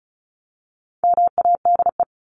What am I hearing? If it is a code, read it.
Morse code, heard as a keyed tone, about 34 wpm, 700 Hz. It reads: MADE
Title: Top 500 Words 35wpm